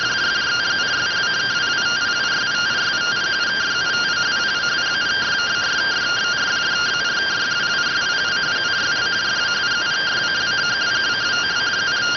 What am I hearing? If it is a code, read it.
Two-tone teleprinter signal, about 170 Hz shift.